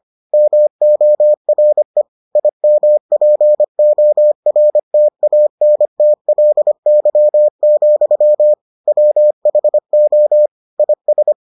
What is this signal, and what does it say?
Morse code, heard as a keyed tone, about 25 wpm, 600 Hz. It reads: MORE IMPORTANTLY, WHO IS